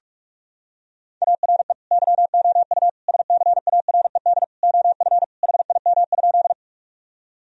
Morse code, heard as a keyed tone, about 46 wpm, 700 Hz. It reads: ARE YOU SCARED OF HIM?
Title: Sentences from Top 1000 Words - Review Part 4 - 45wpm